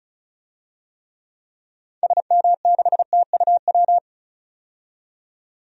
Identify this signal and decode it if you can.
Morse code, heard as a keyed tone, about 34 wpm, 700 Hz. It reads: SM6TUW
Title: DXCC-100 Call Signs - 6 Characters - 35wpm